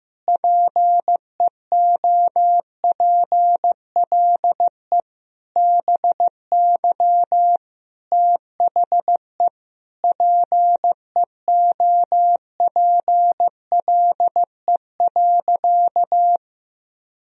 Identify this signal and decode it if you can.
Morse code, heard as a keyed tone, about 15 wpm, 700 Hz. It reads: PEOPLE BY THE PEOPLE.
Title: Sentences from Top 1000 Words - Review Part 3 - 15wpm